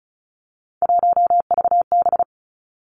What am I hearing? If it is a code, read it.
Morse code, heard as a keyed tone, about 36 wpm, 700 Hz. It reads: 1VB
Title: Three Challenging Letter-Number Combinations 35wpm